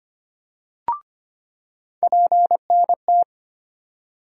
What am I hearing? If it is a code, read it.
Morse code, heard as a keyed tone, about 25 wpm, 700 Hz. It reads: PNT
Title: USA Call Sign Postfixes - 3 Letters 25wpm